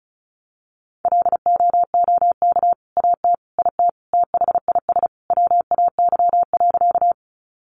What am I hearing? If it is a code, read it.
Morse code, heard as a keyed tone, about 34 wpm, 700 Hz. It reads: LOOK AT IT THIS WAY.
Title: Sentences from Top 100 Words 35wpm